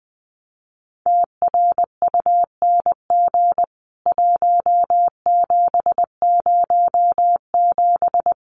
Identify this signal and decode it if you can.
Morse code, heard as a keyed tone, about 20 wpm, 700 Hz. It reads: TRUNG 1707
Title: CWT Exchange 20wpm